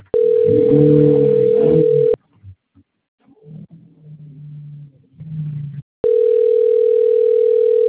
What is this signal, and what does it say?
Ringback tone (North America), 440 + 480 Hz.